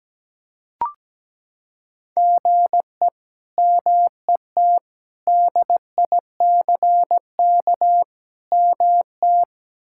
Morse code, heard as a keyed tone, about 17 wpm, 700 Hz. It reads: GE MET DICK MT
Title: K1USN SST QSO Practice - 17wpm